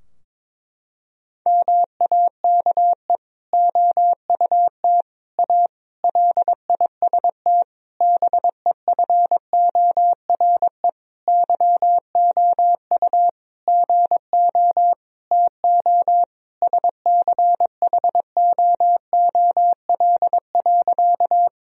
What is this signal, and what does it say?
Morse code, heard as a keyed tone, about 22 wpm, 700 Hz. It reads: MAKE OUT A LIST BEFORE YOU GO TO SCHOOL.
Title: Sentences from Top 1000 Words - Review Part 2 - 22wpm